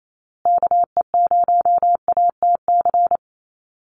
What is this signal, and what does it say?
Morse code, heard as a keyed tone, about 28 wpm, 700 Hz. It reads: KE0ATC
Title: USA Call Signs Repeated 3x 28wpm